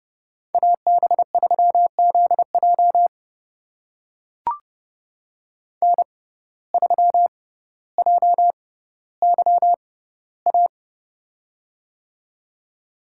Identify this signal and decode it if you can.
Morse code, heard as a keyed tone, about 30 wpm, spaced as for 15 wpm, 700 Hz. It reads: AB3ZJ N3JYA
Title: USA Call Signs - 5 Characters - 6x Character Spacing - 30wpm